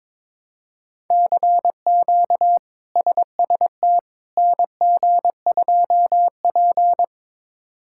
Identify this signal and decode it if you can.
Morse code, heard as a keyed tone, about 22 wpm, 700 Hz. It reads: CQ SST NG2P
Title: K1USN SST QSO Practice - 22wpm